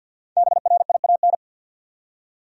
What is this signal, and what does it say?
Morse code, heard as a keyed tone, about 52 wpm, 700 Hz. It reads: BRIAN
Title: USA Common Names - 50wpm